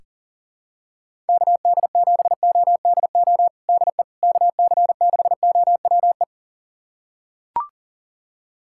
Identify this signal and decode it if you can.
Morse code, heard as a keyed tone, about 40 wpm, 700 Hz. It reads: KD7ODO DE KC6OWE
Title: USA Callsign DE Callsign - 6 Characters - 40wpm